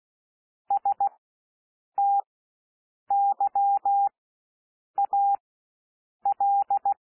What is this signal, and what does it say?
Morse code, heard as a keyed tone, about 16 wpm, spaced as for 7 wpm, 800 Hz. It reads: STYAL
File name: cwmp3 / W1 SOUND WORDS 16 wpm